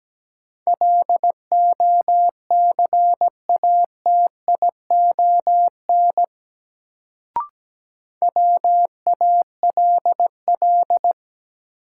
Morse code, heard as a keyed tone, about 17 wpm, 700 Hz. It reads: LOCATION WALL
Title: Sentences from Top 1000 Words - Review Part 6 - New Word Drill - 17wpm